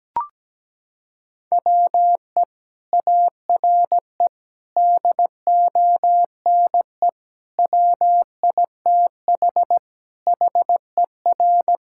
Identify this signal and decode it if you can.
Morse code, heard as a keyed tone, about 17 wpm, 700 Hz. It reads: WE ARE DONE WITH HER
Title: Sentences from Top 1600 Words 17wpm